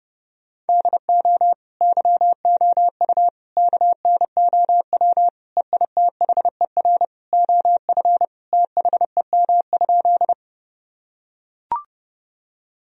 Morse code, heard as a keyed tone, about 30 wpm, 700 Hz. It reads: DO YOU KNOW EITHER OF THEM?
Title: Sentences from Top 1000 Words - Review Part 4 - 30wpm